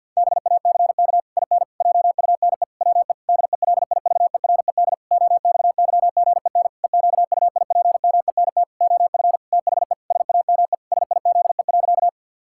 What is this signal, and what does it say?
Morse code, heard as a keyed tone, about 46 wpm, 700 Hz. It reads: BACK IN JUNE WE DELIVERED OXYGEN EQUIPMENT OF THE SAME SIZE.
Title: Pangram Challenge v2 - 50wpm